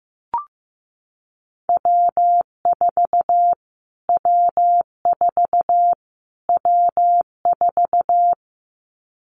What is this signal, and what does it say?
Morse code, heard as a keyed tone, about 15 wpm, 700 Hz. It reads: W4 W4 W4
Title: Callsign Prefixes for Top 25 DXCC - Repeated 3x - 15wpm